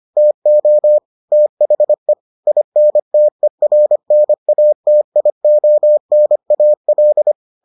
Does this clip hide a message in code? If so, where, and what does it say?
Yes: Morse code from 0.2 to 7.3 s, heard as a keyed tone, about 25 wpm, 600 Hz. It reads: TO THE INTERNATIONAL